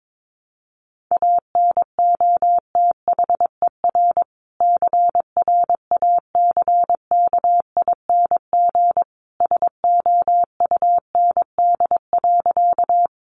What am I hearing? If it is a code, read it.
Morse code, heard as a keyed tone, about 22 wpm, 700 Hz. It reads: ANOTHER CRACKING SOUND.
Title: A Sound Of Thunder 22wpm